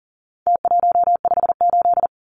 Morse code, heard as a keyed tone, about 40 wpm, 700 Hz. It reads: T158
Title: Numbers - Contest Style - 4-Digit 40wpm